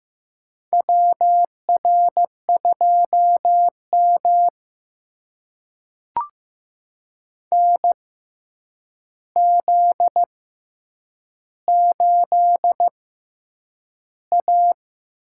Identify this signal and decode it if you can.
Morse code, heard as a keyed tone, about 15 wpm, spaced as for 8 wpm, 700 Hz. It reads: WR2M NZ8A
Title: USA Call Signs - 4 Characters - 6x Character Spacing - 15wpm